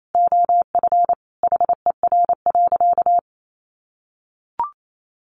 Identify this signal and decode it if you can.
Morse code, heard as a keyed tone, about 28 wpm, 700 Hz. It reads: OF HER.
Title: Sentences from Top 700 Words 28wpm